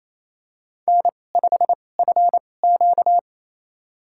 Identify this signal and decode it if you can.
Morse code, heard as a keyed tone, about 28 wpm, spaced as for 20 wpm, 700 Hz. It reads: N5FQ
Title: USA Call Signs - 2x Character Spacing - 28wpm